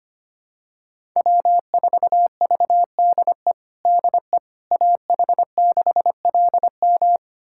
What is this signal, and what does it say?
Morse code, heard as a keyed tone, about 25 wpm, 700 Hz. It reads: W4VDE DE AH6LM
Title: USA Callsign DE Callsign - 5 Characters - 25wpm